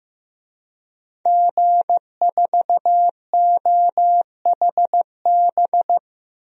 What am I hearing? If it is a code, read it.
Morse code, heard as a keyed tone, about 15 wpm, 700 Hz. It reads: G4OHB
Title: DXCC-100 Call Signs - 5 Characters - Repeated 2x - 15wpm